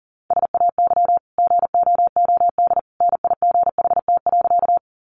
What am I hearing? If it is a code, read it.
Morse code, heard as a keyed tone, about 40 wpm, 700 Hz. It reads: SAY GOOD NIGHT.
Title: Sentences from Top 300 Words 40wpm